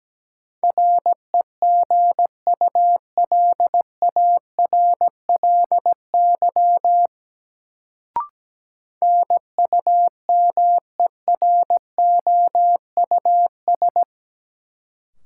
Morse code, heard as a keyed tone, about 17 wpm, 700 Hz. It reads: REGULARLY NUMEROUS
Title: Sentences from Top 1400 Words - New Word Drill - 17wpm